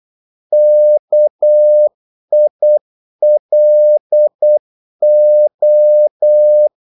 Morse code, heard as a keyed tone, about 8 wpm, 600 Hz. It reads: KILO
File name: Korte ord DK 8wpm